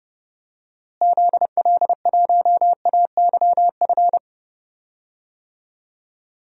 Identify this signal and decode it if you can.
Morse code, heard as a keyed tone, about 31 wpm, 700 Hz. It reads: ZL1AYF
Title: DXCC-100 Call Signs - 6 Characters - Repeated 3x - 30wpm